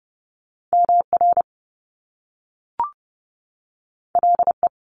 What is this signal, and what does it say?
Morse code, heard as a keyed tone, about 30 wpm, 700 Hz. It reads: MR LE
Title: Top Two Letter Combinations in Words 30wpm